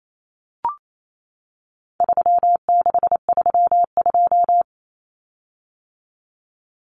Morse code, heard as a keyed tone, about 28 wpm, 700 Hz. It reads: 3632
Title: Common Cut Numbers - 4-Digit 28wpm